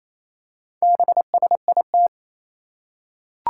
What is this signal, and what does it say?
Morse code, heard as a keyed tone, about 27 wpm, 700 Hz. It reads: BSIT
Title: Four Challenging Letter-Number Combinations 28wpm